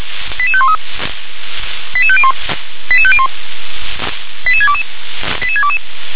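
Five-tone selective call (ZVEI), heard as quick runs of five tones, tones of about 70 ms. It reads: 88513 88501 88501 88622 88622
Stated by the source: ZVEI five-tone call received by SDR.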